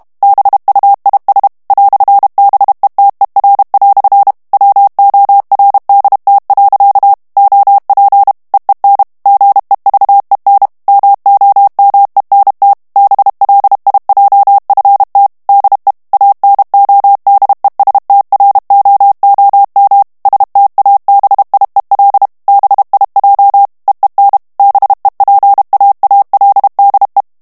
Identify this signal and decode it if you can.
Morse code, heard as a keyed tone, about 32 wpm, 800 Hz. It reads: BUIS "BETER" WORDT. OP EEN GEVEN MOMENT BLIJFT DE ANODESTROOM STABIEL BIJ EEN BEPAALDE
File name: HOLLANDAIS texte clair 2 vitesse 30 mots minute